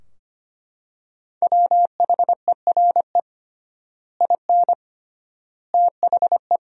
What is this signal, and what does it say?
Morse code, heard as a keyed tone, about 25 wpm, 700 Hz. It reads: WHERE IN THE
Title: 3x Word-Spacing - Sentences from Top 200 Words - 25wpm